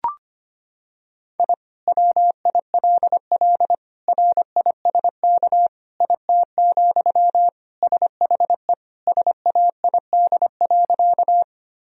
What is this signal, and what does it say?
Morse code, heard as a keyed tone, about 25 wpm, 700 Hz. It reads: I WILL RISK IT, SHE SAID.
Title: Sentences from Top 1000 Words - Review Part 4 - 25wpm